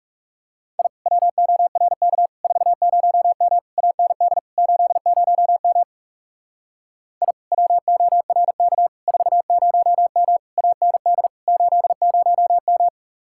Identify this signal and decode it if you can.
Morse code, heard as a keyed tone, about 36 wpm, 700 Hz. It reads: I WORK 40M AND 80M I WORK 40M AND 80M
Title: Speed-Racing Common QSO Phrases Encore - 30wpm